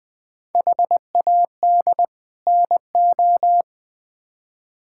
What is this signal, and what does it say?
Morse code, heard as a keyed tone, about 21 wpm, 700 Hz. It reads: HAD NO
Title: Build A Phrase - 5-Word Phrases - 20wpm